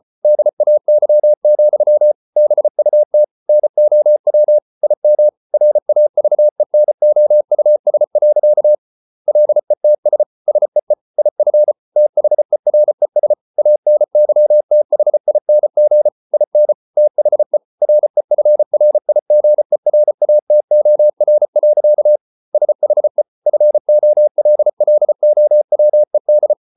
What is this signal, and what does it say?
Morse code, heard as a keyed tone, about 34 wpm, 600 Hz. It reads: DAY, BUT NOW IM RAVENOUS. LETS SEE IF THERES ANYTHING IN THE REFRIGERATOR. SHE FOLLOWED